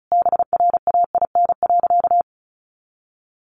Morse code, heard as a keyed tone, about 34 wpm, 700 Hz. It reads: BRAIN.
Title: Sentences from Top 1100 Words 35wpm